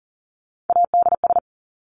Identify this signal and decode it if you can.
Morse code, heard as a keyed tone, about 40 wpm, 700 Hz. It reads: ADS
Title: Abbreviations 40wpm